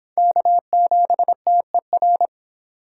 Morse code, heard as a keyed tone, about 26 wpm, 700 Hz. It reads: K7TER